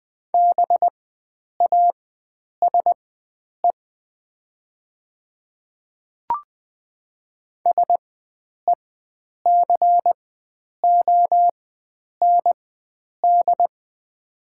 Morse code, heard as a keyed tone, about 20 wpm, spaced as for 9 wpm, 700 Hz. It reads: BASE SECOND
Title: 4x Character Spacing - Top 300 Words - 20wpm